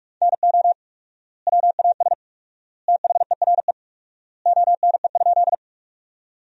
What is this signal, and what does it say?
Morse code, heard as a keyed tone, about 46 wpm, 700 Hz. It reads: NO WAS THERE ONE?
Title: 4x Word-Spacing - Sentences from Top 100 Words - 45wpm